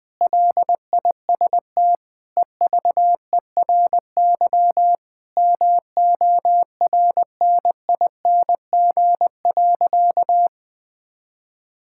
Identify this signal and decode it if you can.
Morse code, heard as a keyed tone, about 20 wpm, 700 Hz. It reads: LIST EVERY MORNING.
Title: Sentences from Top 1000 Words - Review Part 6 - 20wpm